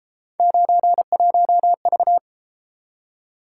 Morse code, heard as a keyed tone, about 32 wpm, 700 Hz. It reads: 91V